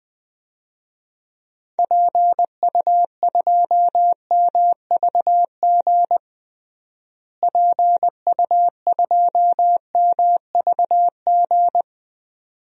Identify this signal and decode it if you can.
Morse code, heard as a keyed tone, about 20 wpm, 700 Hz. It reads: PU2MVG PU2MVG
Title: DXCC-100 Call Signs - 6 Characters - Repeated 2x - 20wpm